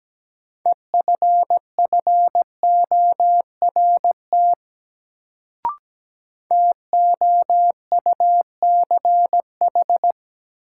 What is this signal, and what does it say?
Morse code, heard as a keyed tone, about 17 wpm, 700 Hz. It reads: EFFORT TOUCH
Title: Sentences from Top 1200 Words - New Word Drill - 17wpm